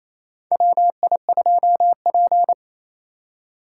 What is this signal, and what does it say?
Morse code, heard as a keyed tone, about 28 wpm, 690 Hz. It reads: WI2P
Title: USA Call Signs - 4 Characters - 2x Character Spacing - 28wpm